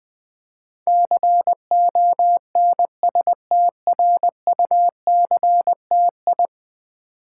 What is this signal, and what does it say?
Morse code, heard as a keyed tone, about 21 wpm, 700 Hz. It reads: CONSTRUCTI
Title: Build A Word - 5 Letters At A Time - 12-Letter Words - 20wpm